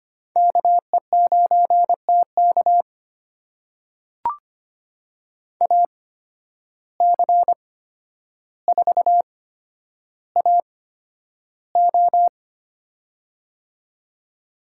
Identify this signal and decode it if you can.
Morse code, heard as a keyed tone, about 25 wpm, spaced as for 11 wpm, 700 Hz. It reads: KE9TK AC4AO
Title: USA Call Signs - 5 Characters - 8x Character Spacing - 25wpm